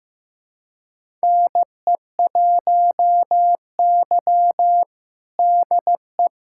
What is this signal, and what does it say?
Morse code, heard as a keyed tone, about 15 wpm, 700 Hz. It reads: NE1Y DE
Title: USA Callsign DE Callsign 15wpm